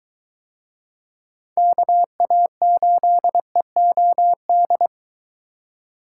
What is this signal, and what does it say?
Morse code, heard as a keyed tone, about 22 wpm, 700 Hz. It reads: KA8EOD